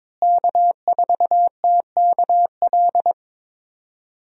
Morse code, heard as a keyed tone, about 21 wpm, 700 Hz. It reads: K4TKL
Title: USA Call Signs - 5 Characters - Repeated 2x - 22wpm